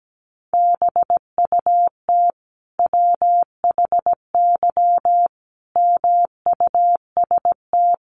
Morse code, heard as a keyed tone, about 17 wpm, 700 Hz. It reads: BUT WHY MUST
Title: Sentences from Top 1300 Words 17wpm